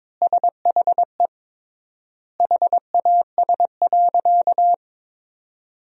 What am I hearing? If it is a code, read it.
Morse code, heard as a keyed tone, about 22 wpm, 700 Hz. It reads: SHE HAS.
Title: 3x Word-Spacing - Sentences from Top 500 Words - 22wpm